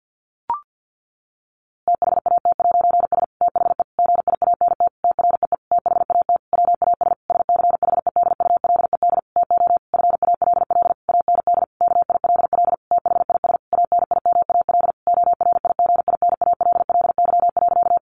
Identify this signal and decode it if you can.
Morse code, heard as a keyed tone, about 48 wpm, 700 Hz. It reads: THAT'S THE GIANT TREE THAT WAS SCHEDULED TO FALL AND KILL THIS ANIMAL ORIGINALLY.
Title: A Sound Of Thunder 50wpm